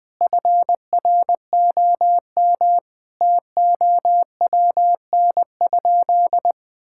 Morse code, heard as a keyed tone, about 20 wpm, 700 Hz. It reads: FROM TOWN?
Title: Sentences from Top 1000 Words - Review Part 5 - 20wpm